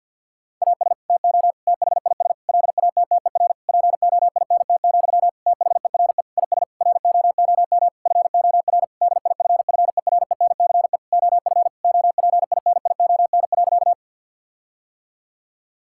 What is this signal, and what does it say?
Morse code, heard as a keyed tone, about 50 wpm, 700 Hz. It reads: AS TO THIS LATTER POINT, THERE IS ROOM FOR DIFFERENCE OF OPINION.
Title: Sentences from Top 1600 Words 50wpm